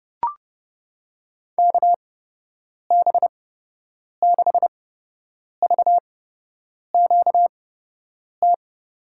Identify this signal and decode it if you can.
Morse code, heard as a keyed tone, about 30 wpm, spaced as for 8 wpm, 700 Hz. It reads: KB6VQT
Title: USA Call Signs - 8x Character Spacing - 30wpm